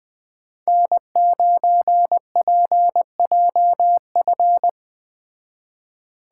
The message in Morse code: N9PJF

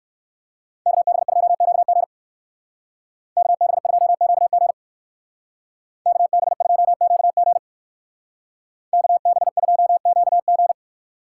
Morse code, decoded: KB2QG KB2QG KB2QG KB2QG

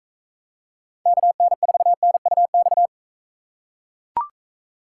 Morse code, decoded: KN4NUX

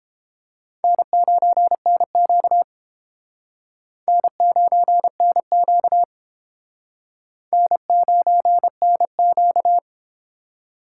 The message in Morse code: N9NQ N9NQ N9NQ

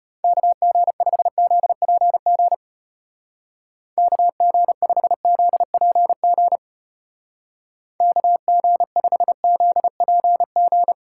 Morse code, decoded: KG5ZPG KG5ZPG KG5ZPG

